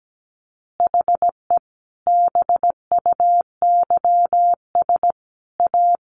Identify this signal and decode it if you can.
Morse code, heard as a keyed tone, about 17 wpm, 700 Hz. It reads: HE BUYS A
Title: Easy Phrases 17wpm